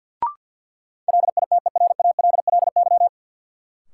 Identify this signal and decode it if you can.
Morse code, heard as a keyed tone, about 52 wpm, 700 Hz. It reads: LITERALLY
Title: Sentences from Top 1400 Words - New Word Drill Alt - 50wpm